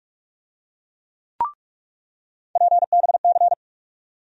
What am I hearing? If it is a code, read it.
Morse code, heard as a keyed tone, about 46 wpm, 700 Hz. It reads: PBC
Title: USA Call Sign Postfixes - 3 Letters 45wpm